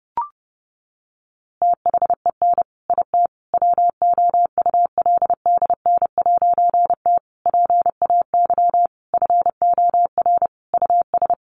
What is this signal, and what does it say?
Morse code, heard as a keyed tone, about 30 wpm, 700 Hz. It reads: THEN IT WOULDN'T PAY FOR US
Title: A Sound Of Thunder 30wpm